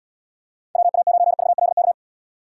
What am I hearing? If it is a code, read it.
Morse code, heard as a keyed tone, about 71 wpm, 700 Hz. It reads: WA9VLK